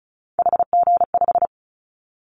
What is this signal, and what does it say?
Morse code, heard as a keyed tone, about 34 wpm, 700 Hz. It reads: HG5
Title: Callsign Prefixes for Top 25 DXCC - Repeated 3x - 35wpm